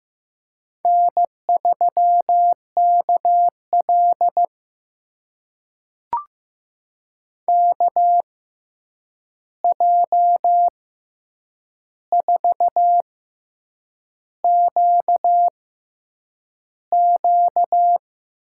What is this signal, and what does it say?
Morse code, heard as a keyed tone, about 15 wpm, spaced as for 7 wpm, 700 Hz. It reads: N3KL KJ4QQ
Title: USA Call Signs - 6x Character Spacing - 15wpm